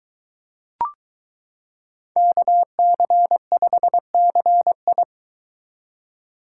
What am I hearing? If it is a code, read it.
Morse code, heard as a keyed tone, about 23 wpm, 700 Hz. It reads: KC5CI